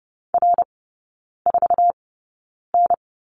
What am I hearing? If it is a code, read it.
Morse code, heard as a keyed tone, about 30 wpm, spaced as for 9 wpm, 700 Hz. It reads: R4N
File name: course / 30:10wpm / 087 Lesson 15e - Number 5 - Three Characters 30wpm